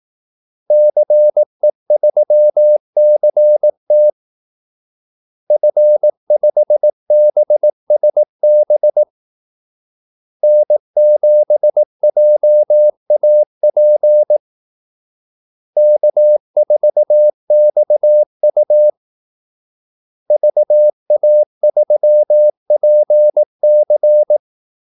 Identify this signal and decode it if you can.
Morse code, heard as a keyed tone, about 18 wpm, 600 Hz. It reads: CE3CT F5BSB N7JAP K4XU VA3PC
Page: Kaldesignaler 18 wpm